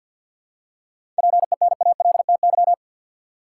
Morse code, heard as a keyed tone, about 47 wpm, 700 Hz. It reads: PENALTY
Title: Seven Letter Words - 50wpm